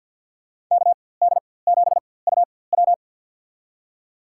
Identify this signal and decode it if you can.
Morse code, heard as a keyed tone, about 53 wpm, spaced as for 24 wpm, 700 Hz. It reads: KD7UW
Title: USA Call Signs - 5 Characters - 4x Character Spacing - 50wpm